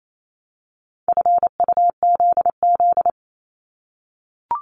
Morse code, heard as a keyed tone, about 28 wpm, 700 Hz. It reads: FUZZ
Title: Multi-letter Phonemes Repeated 3x 28wpm